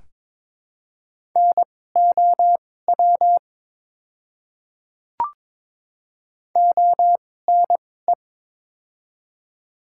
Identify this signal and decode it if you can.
Morse code, heard as a keyed tone, about 22 wpm, spaced as for 16 wpm, 700 Hz. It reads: NOW ONE
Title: 2x Character Spacing - Top 100 Words - 22wpm